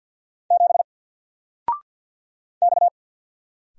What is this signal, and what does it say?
Morse code, heard as a keyed tone, about 50 wpm, 700 Hz. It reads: 7 X
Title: Single Letter-Number 50wpm